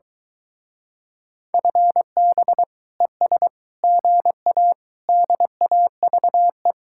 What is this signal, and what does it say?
Morse code, heard as a keyed tone, about 23 wpm, 700 Hz. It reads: FB ES GA DAVE